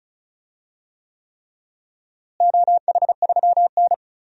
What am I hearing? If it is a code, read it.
Morse code, heard as a keyed tone, about 34 wpm, 700 Hz. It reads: OH3N
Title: DXCC-100 Call Signs - 4 Characters - Repeated 3x - 35wpm